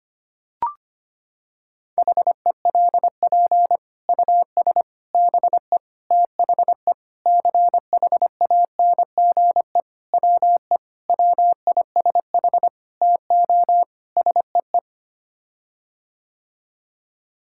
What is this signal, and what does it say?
Morse code, heard as a keyed tone, about 25 wpm, 700 Hz. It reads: HELP US BE THE CHANGE WE WISH TO SEE
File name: Sets of 9 Words - 25wpm